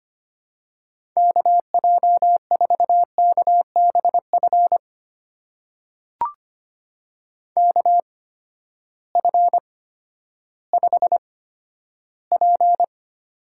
Morse code, heard as a keyed tone, about 25 wpm, spaced as for 12 wpm, 700 Hz. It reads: KJ4KBF KF5P
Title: USA Call Signs - 6 Characters - 8x Character Spacing - 25wpm